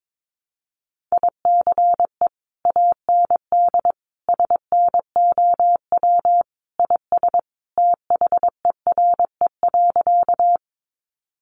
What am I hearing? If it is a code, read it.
Morse code, heard as a keyed tone, about 22 wpm, 700 Hz. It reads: ICE AND SNOW IS THERE.
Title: Sentences from Top 1000 Words 22wpm